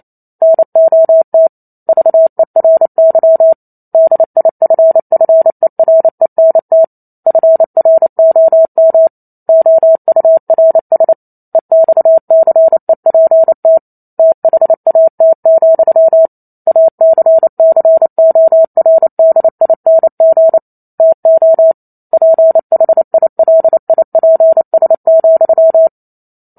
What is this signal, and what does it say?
Morse code, heard as a keyed tone, about 28 wpm, 650 Hz. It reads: NOT VERY DIFFERENT FROM OURS EXCEPT THAT, ACCORDING TO PHILIPS,